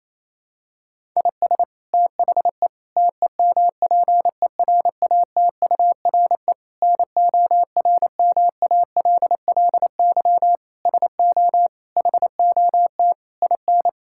Morse code, heard as a keyed tone, about 28 wpm, 700 Hz. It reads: IS THE TEMPERATURE NORMALLY SO HOT IN
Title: Sentences from Top 1000 Words - Review Part 6 - 28wpm